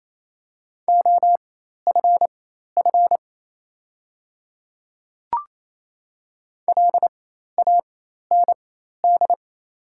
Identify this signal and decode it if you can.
Morse code, heard as a keyed tone, about 28 wpm, spaced as for 13 wpm, 700 Hz. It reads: OFF LAND
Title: 4x Character Spacing - Top 300 Words - 28wpm